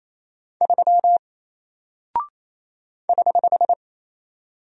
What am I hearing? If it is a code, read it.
Morse code, heard as a keyed tone, about 27 wpm, 700 Hz. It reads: 3 <HH>
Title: Single Character (Letter:Number:Punctuation:Prosign) 28wpm